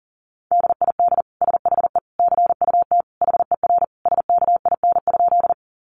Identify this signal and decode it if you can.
Morse code, heard as a keyed tone, about 40 wpm, 700 Hz. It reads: DID SHE CUT HER SKIN?
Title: Sentences from Top 1300 Words 40wpm